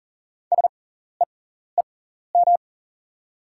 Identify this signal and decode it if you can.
Morse code, heard as a keyed tone, about 40 wpm, spaced as for 14 wpm, 690 Hz. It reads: SEEM
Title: 6x Character Spacing - Top 300 Words - 40wpm